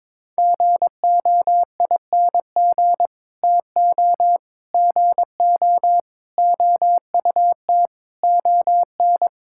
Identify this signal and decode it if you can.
Morse code, heard as a keyed tone, about 22 wpm, 700 Hz. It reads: GOING TO GO OUT ON
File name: practice / 22wpm / Sets of 8 Words - 22wpm